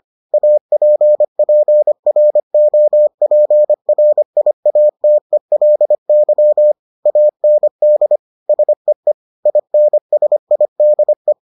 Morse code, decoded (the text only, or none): APPROPRIATELY AND SEE INSIDE